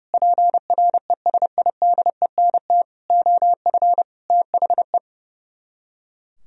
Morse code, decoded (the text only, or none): PRESIDENT OF THE